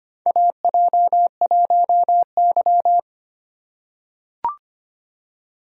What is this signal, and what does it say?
Morse code, heard as a keyed tone, about 25 wpm, 700 Hz. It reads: AJ1Y
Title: USA Call Signs - 4 Characters - 2x Character Spacing - 25wpm